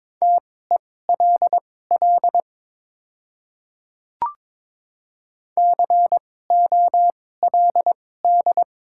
Morse code, heard as a keyed tone, about 22 wpm, spaced as for 16 wpm, 690 Hz. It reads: TELL COLD
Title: 2x Character Spacing - Top 500 Words - 22wpm